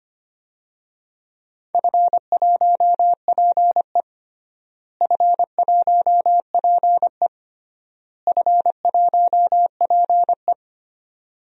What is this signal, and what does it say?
Morse code, heard as a keyed tone, about 25 wpm, 700 Hz. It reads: F1PE F1PE F1PE
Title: DXCC-100 Call Signs - 4 Characters - Repeated 3x - 25wpm